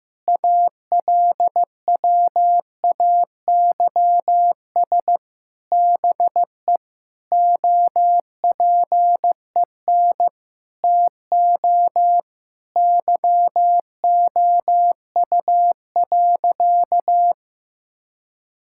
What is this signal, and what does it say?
Morse code, heard as a keyed tone, about 15 wpm, 700 Hz. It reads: ALWAYS BE OPEN TO YOU.
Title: Sentences from Top 1600 Words 15wpm